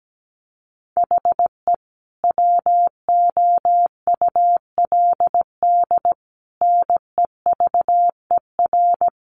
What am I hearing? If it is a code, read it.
Morse code, heard as a keyed tone, about 17 wpm, 700 Hz. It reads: HE WOULD NEVER
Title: Sentences from Top 1200 Words 17wpm